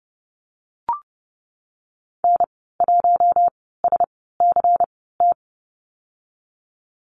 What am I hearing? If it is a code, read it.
Morse code, heard as a keyed tone, about 30 wpm, 700 Hz. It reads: N 1 S C T
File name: Lesson 19g - Letter w - Call Signs 30wpm